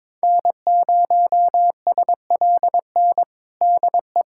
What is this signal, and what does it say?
Morse code, heard as a keyed tone, about 22 wpm, 700 Hz. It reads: N0SLN DE
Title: USA Callsign DE Callsign - 5 Characters - 22wpm